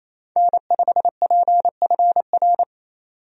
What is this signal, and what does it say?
Morse code, heard as a keyed tone, about 28 wpm, 700 Hz. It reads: N5PFR
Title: USA Call Signs - 5 Characters - 28wpm